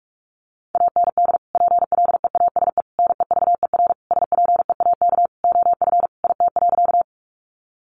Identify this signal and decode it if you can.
Morse code, heard as a keyed tone, about 44 wpm, 700 Hz. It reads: AND PLEASE NEVER SPEAK OF IT.
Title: Sentences from Top 1500 Words 45wpm